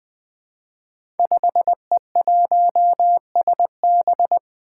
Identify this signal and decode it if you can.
Morse code, heard as a keyed tone, about 20 wpm, 700 Hz. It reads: 5E1SB
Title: Five Challenging Letter-Number Combinations 20wpm